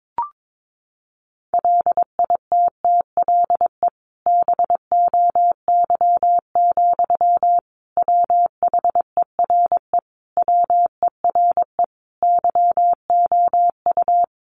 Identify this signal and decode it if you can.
Morse code, heard as a keyed tone, about 22 wpm, 700 Hz. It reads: LITTLE BOY, WHERE WERE YOU
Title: Sentences from Top 1000 Words 22wpm